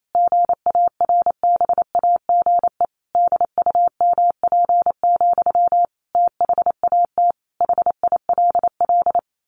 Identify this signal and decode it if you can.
Morse code, heard as a keyed tone, about 28 wpm, 700 Hz. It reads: GARBAGE DUMP, THAT HILL
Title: A Sound Of Thunder 28wpm